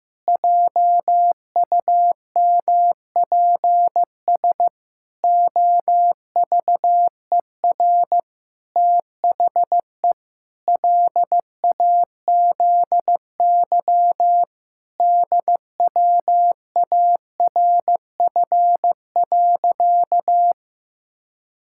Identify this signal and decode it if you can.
Morse code, heard as a keyed tone, about 16 wpm, 700 Hz. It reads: JUMPS OVER THE LAZY DWARF.
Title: Pangram Challenge v2 - 15wpm